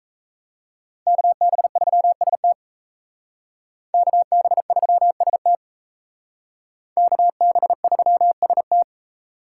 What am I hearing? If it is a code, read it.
Morse code, heard as a keyed tone, about 35 wpm, 700 Hz. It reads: KB3ST KB3ST KB3ST
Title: Speed Racing - USA 2x2 Call Signs 28wpm